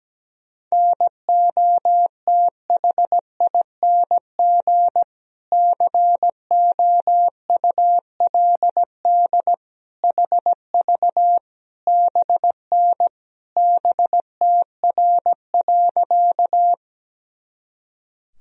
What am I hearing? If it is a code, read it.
Morse code, heard as a keyed tone, about 17 wpm, 700 Hz. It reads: NOTHING COULD HV BN BTR.
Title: Sentences from Top 500 Words and Abbreviations 17wpm